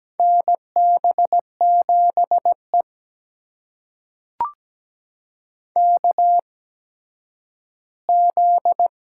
Morse code, heard as a keyed tone, about 17 wpm, spaced as for 10 wpm, 700 Hz. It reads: NB7E KZ
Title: USA Call Signs - 4 Characters - 8x Character Spacing - 17wpm